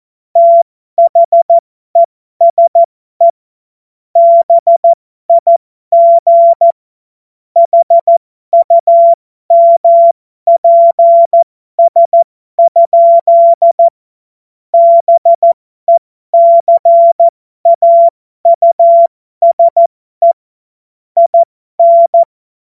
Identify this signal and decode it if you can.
Morse code, heard as a keyed tone, about 14 wpm, spaced as for 12 wpm, 680 Hz. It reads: THESE BIG HUMPS? BECAUSE IN
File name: Morsekurs / Witze